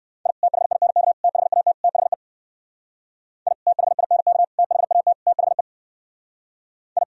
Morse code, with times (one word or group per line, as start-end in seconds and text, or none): 0.3-0.3 I
0.4-1.1 THINK
1.2-1.7 THAT
1.8-2.1 THE
3.5-3.5 I
3.7-4.4 THINK
4.6-5.1 THAT
5.3-5.6 THE
7.0-7.0 I